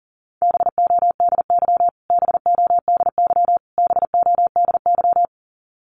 Morse code, heard as a keyed tone, about 40 wpm, 700 Hz. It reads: BODY BODY BODY
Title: Top 500 Words Repeated 40wpm